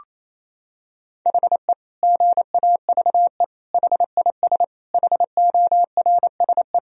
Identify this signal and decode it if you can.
Morse code, heard as a keyed tone, about 28 wpm, 700 Hz. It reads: HE GAVE HIS HORSE
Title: Sentences from Top 1000 Words - Review Part 1 - 28wpm